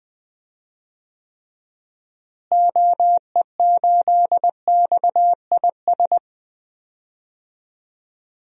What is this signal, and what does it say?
Morse code, heard as a keyed tone, about 21 wpm, 700 Hz. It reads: OE8XIS
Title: DXCC-100 Call Signs - 6 Characters - Repeated 3x - 20wpm